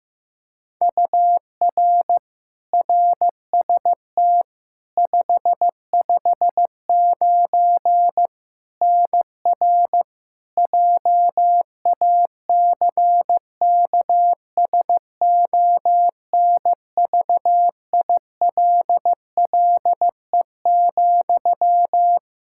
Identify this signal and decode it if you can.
Morse code, heard as a keyed tone, about 15 wpm, 700 Hz. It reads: UR RST 559 NR JACKSONVILLE,
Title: Practice QSO - Exchange 1 - Beginning - 15wpm